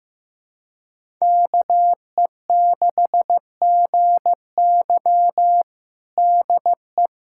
Morse code, heard as a keyed tone, about 15 wpm, 700 Hz. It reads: KE6GY DE
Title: USA Callsign DE Callsign 15wpm